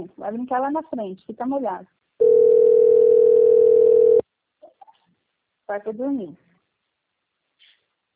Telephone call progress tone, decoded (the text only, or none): ringback tone